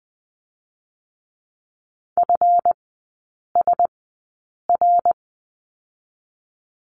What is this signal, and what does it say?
Morse code, heard as a keyed tone, about 20 wpm, spaced as for 8 wpm, 700 Hz. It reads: FSR
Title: 138 Lesson 23e - Letter f - Three Characters 20wpm